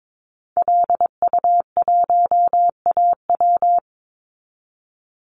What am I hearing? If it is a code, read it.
Morse code, heard as a keyed tone, about 22 wpm, 700 Hz. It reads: LU1AW
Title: Call Signs Repeated 2x - 22wpm